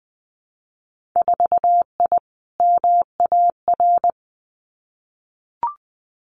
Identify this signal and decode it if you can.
Morse code, heard as a keyed tone, about 20 wpm, 700 Hz. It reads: 4I MAR
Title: Winter Field Day Exchanges Encore 20wpm